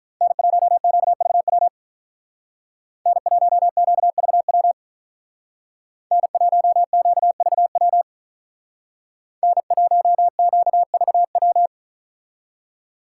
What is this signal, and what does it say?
Morse code, heard as a keyed tone, about 43 wpm, 700 Hz. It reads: N1QVW N1QVW N1QVW N1QVW
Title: Speed Racing - USA 1x3 Call Signs 35wpm